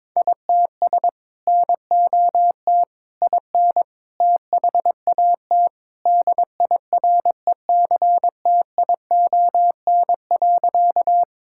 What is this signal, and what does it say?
Morse code, heard as a keyed tone, about 22 wpm, 700 Hz. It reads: ITS NOT IN THAT DIRECTION.
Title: Sentences from Top 1000 Words - Review Part 5 - 22wpm